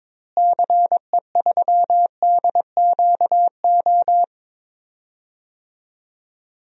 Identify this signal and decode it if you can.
Morse code, heard as a keyed tone, about 22 wpm, 700 Hz. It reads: CE3DQO
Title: DXCC-100 Call Signs - 6 Characters - Repeated 2x - 22wpm